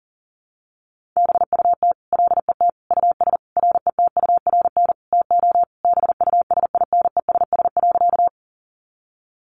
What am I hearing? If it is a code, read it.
Morse code, heard as a keyed tone, about 40 wpm, 700 Hz. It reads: BUT LET US RETURN TO BUSINESS.
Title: Sentences from Top 800 Words 40wpm